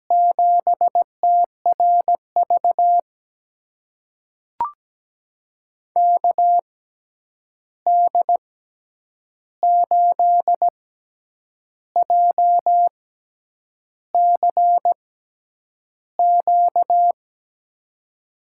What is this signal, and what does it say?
Morse code, heard as a keyed tone, about 17 wpm, spaced as for 8 wpm, 700 Hz. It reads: 7TRV KD8JCQ
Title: USA Call Signs - 6 Characters - 6x Character Spacing - 17wpm